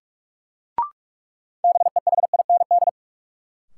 Morse code, heard as a keyed tone, about 45 wpm, 700 Hz. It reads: BEHIND
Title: Sentences from Top 500 Words - New Word Drill - 45wpm